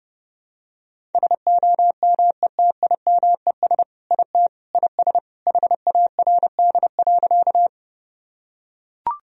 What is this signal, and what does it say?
Morse code, heard as a keyed tone, about 30 wpm, 700 Hz. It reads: SOMETIMES IT IS HARD.
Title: Sentences from Top 1000 Words - Review Part 3 - 30wpm